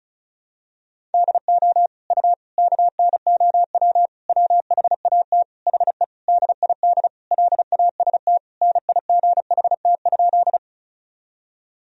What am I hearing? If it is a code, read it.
Morse code, heard as a keyed tone, about 36 wpm, 700 Hz. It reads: DO U KNOW WHAT HE DID LAST NIGHT?
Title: Sentences from Top 300 Words and Abbreviations 35wpm